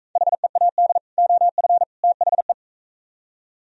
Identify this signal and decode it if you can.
Morse code, heard as a keyed tone, about 42 wpm, 700 Hz. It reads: HEAD OF THE